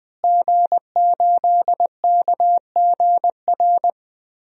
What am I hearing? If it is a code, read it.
Morse code, heard as a keyed tone, about 20 wpm, 700 Hz. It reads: G8KGR
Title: DXCC-100 Call Signs - 5 Characters - Repeated 2x - 20wpm